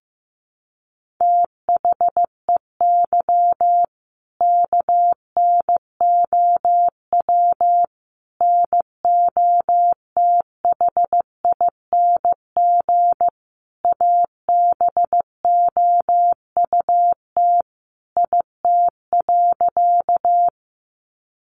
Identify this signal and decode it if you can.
Morse code, heard as a keyed tone, about 15 wpm, 700 Hz. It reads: THEY KNOW NOTHING ABOUT IT.
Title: Sentences from Top 500 Words 15wpm